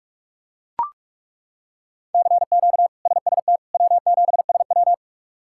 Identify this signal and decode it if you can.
Morse code, heard as a keyed tone, about 41 wpm, 700 Hz. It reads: CQ SST W7SW
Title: K1USN SST QSO Practice - 45wpm